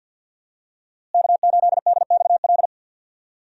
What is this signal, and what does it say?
Morse code, heard as a keyed tone, about 49 wpm, 700 Hz. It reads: K8DXL
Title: USA Call Signs - 5 Characters - Repeated 3x - 50wpm